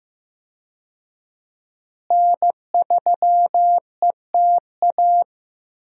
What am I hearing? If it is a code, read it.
Morse code, heard as a keyed tone, about 15 wpm, 700 Hz. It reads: N3ETA
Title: USA Call Signs - 5 Characters - Repeated 2x - 15wpm